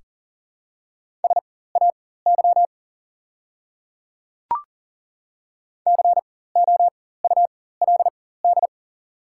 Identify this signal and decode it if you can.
Morse code, heard as a keyed tone, about 40 wpm, spaced as for 19 wpm, 700 Hz. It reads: SAY COULD
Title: 4x Character Spacing - Top 300 Words - 40wpm